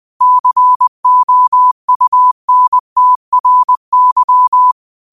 Morse code, heard as a keyed tone, about 20 wpm, 1000 Hz. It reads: COUNTRY
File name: qotdfiles / 20wpm